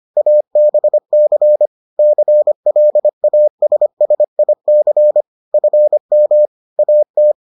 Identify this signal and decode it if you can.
Morse code, heard as a keyed tone, about 25 wpm, 600 Hz. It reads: ABC CLASSIC FM AT